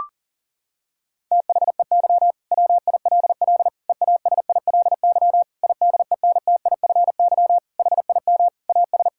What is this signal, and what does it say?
Morse code, heard as a keyed tone, about 40 wpm, 700 Hz. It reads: THEY WILL EASILY IDENTIFY HIM AS
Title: Sentences from Top 1000 Words - Review Part 6 - 40wpm